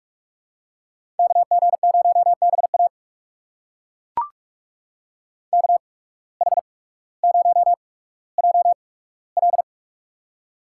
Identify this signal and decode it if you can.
Morse code, heard as a keyed tone, about 42 wpm, spaced as for 19 wpm, 700 Hz. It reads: KG0BA KH0JL
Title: USA Call Signs - 5 Characters - 8x Character Spacing - 45wpm